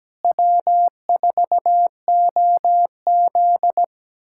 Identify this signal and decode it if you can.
Morse code, heard as a keyed tone, about 17 wpm, 700 Hz. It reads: W4OZ
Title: USA Callsign DE Callsign - 4 Characters - 17wpm